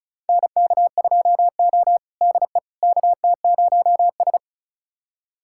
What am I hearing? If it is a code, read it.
Morse code, heard as a keyed tone, about 36 wpm, 700 Hz. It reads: NK2O DE KT0S
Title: USA Callsign DE Callsign - 4 Characters - 35wpm